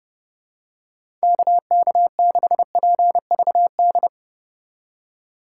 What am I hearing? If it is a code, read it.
Morse code, heard as a keyed tone, about 32 wpm, 700 Hz. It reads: KK6PVD
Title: USA Call Signs - 6 Characters - 2x Character Spacing - 30wpm